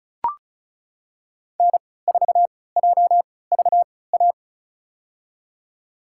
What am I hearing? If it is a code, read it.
Morse code, heard as a keyed tone, about 36 wpm, spaced as for 20 wpm, 690 Hz. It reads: N4JVA
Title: USA Call Signs - 5 Characters - 3x Character Spacing - 35wpm